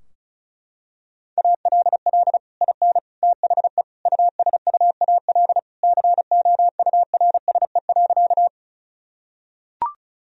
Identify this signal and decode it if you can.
Morse code, heard as a keyed tone, about 34 wpm, 700 Hz. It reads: ALL IN THE USUAL COURSE.
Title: Sentences from Top 1000 Words - Review Part 2 - 35wpm